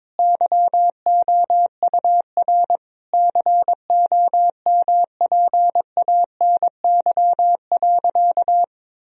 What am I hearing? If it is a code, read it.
Morse code, heard as a keyed tone, about 22 wpm, 700 Hz. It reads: YOUR COMPANY.
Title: Sentences from Top 1000 Words - Review Part 3 - 22wpm